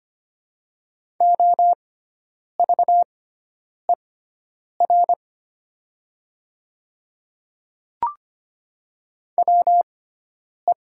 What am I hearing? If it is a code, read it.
Morse code, heard as a keyed tone, about 25 wpm, spaced as for 9 wpm, 700 Hz. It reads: OVER WE
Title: 6x Character Spacing - Top 100 Words - 25wpm